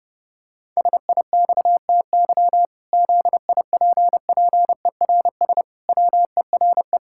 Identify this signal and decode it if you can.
Morse code, heard as a keyed tone, about 30 wpm, 700 Hz. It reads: SIXTY ZIPPERS WERE
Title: Pangram Challenge v2 - 30wpm